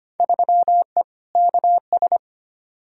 Morse code, heard as a keyed tone, about 25 wpm, 700 Hz. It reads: 3E KS
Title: Field Day Exchanges Encore - 25wpm